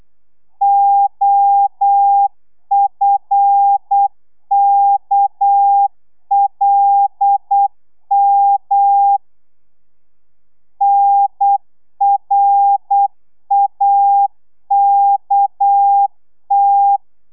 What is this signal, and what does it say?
Morse code, heard as a keyed tone, about 8 wpm, 800 Hz. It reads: OFKLM NRAKT
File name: telegrafia / i2tfj / Lez6 35